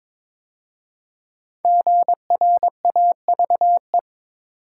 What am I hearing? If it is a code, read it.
Morse code, heard as a keyed tone, about 22 wpm, 700 Hz. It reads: GRAVE
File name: 5 Letter Words - First 3 Letters Given - 22wpm